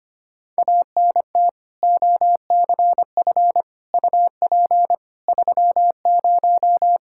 Morse code, heard as a keyed tone, about 25 wpm, 700 Hz. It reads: ANT OCF UP 30
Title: Practice QSO - Exchange 2 - Info - 25wpm